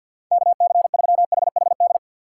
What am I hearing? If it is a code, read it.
Morse code, heard as a keyed tone, about 51 wpm, 700 Hz. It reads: KC3HHD
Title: USA Call Signs - 4x Character Spacing - 50wpm